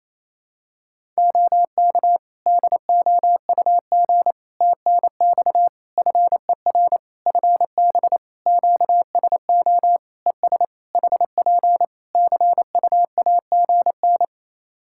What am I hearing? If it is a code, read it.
Morse code, heard as a keyed tone, about 28 wpm, 700 Hz. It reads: OK DOUG TNX FER FB QSO ES HP CUAGN
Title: Practice QSO - Exchange 4 - Ending - 28wpm